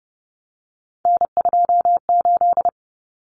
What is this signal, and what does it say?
Morse code, heard as a keyed tone, about 30 wpm, 700 Hz. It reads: N28
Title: Common Cut Numbers - 3-Digit 30wpm